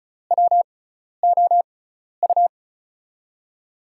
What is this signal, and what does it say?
Morse code, heard as a keyed tone, about 36 wpm, spaced as for 12 wpm, 700 Hz. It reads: WOU
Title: USA Call Signs - 6 Characters - 6x Character Spacing - 35wpm